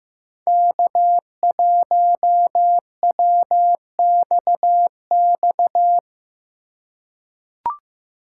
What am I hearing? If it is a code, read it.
Morse code, heard as a keyed tone, about 15 wpm, 700 Hz. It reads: K1WXX
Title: USA Call Signs - 5 Characters - 8x Character Spacing - 15wpm